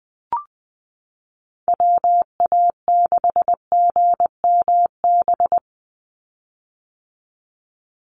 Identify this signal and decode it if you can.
Morse code, heard as a keyed tone, about 20 wpm, 700 Hz. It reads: WA6GMB
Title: USA Call Signs 20wpm